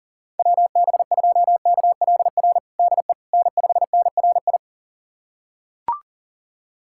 Morse code, heard as a keyed tone, about 37 wpm, 700 Hz. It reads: WB2KLR DE N5NRI
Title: USA Callsign DE Callsign 40wpm